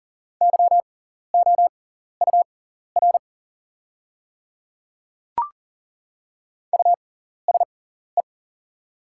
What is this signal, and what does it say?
Morse code, heard as a keyed tone, about 38 wpm, spaced as for 14 wpm, 690 Hz. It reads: YOUR USE
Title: 6x Character Spacing - Top 100 Words - 40wpm